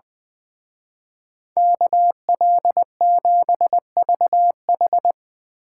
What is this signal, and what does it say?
Morse code, heard as a keyed tone, about 21 wpm, 700 Hz. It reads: KL7VH
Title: Speed Racing - USA 2x2 Call Signs 20wpm